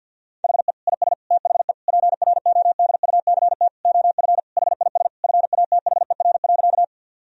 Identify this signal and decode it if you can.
Morse code, heard as a keyed tone, about 50 wpm, 700 Hz. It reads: HE IS THE PRODUCT OF HIS FATHER.
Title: Sentences from Top 1000 Words - Review Part 2 - 50wpm